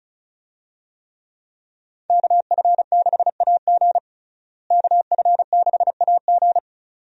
Morse code, heard as a keyed tone, about 34 wpm, 700 Hz. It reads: KF6AG KF6AG
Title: USA Call Signs - 5 Characters - Repeated 2x - 35wpm